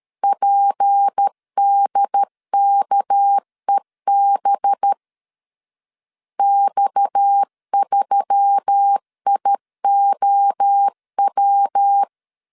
Morse code, heard as a keyed tone, about 13 wpm, 800 Hz. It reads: PDKEB X3IOW
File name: LEZIONI / 6 PKX0 / 6 PKX 12wpm 2024